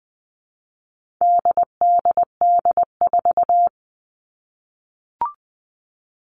Morse code, heard as a keyed tone, about 20 wpm, 700 Hz. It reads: DDD4
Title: All Cut Numbers - 4-Digit 20wpm